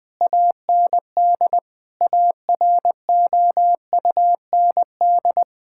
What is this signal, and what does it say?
Morse code, heard as a keyed tone, about 21 wpm, 700 Hz. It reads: AND AROUND
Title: Sentences with 8-Word Phrases - 20wpm